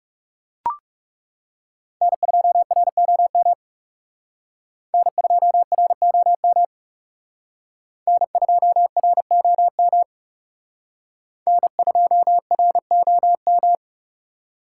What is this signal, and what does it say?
Morse code, heard as a keyed tone, about 38 wpm, 700 Hz. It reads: N2ROM N2ROM N2ROM N2ROM